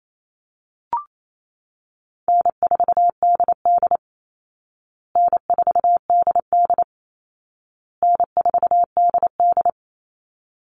Morse code, heard as a keyed tone, about 28 wpm, 700 Hz. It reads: N4DD N4DD N4DD
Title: Easy Call Signs Repeated 3x - 28wpm